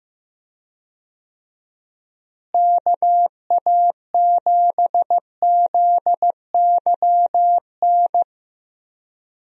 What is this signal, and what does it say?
Morse code, heard as a keyed tone, about 15 wpm, 700 Hz. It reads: KA7ZYN
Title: USA Call Signs - 6 Characters - Repeated 2x - 15wpm